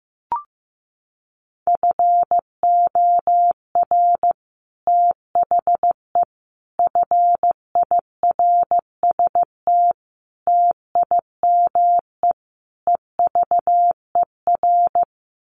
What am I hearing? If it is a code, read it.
Morse code, heard as a keyed tone, about 15 wpm, 700 Hz. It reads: FOR THE FIRST TIME EVER
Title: Sets of 5 Words Encore2 15wpm